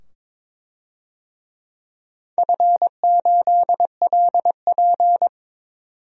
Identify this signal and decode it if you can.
Morse code, heard as a keyed tone, about 22 wpm, 700 Hz. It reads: F8LP
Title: DXCC-100 Call Signs - 4 Characters - 22wpm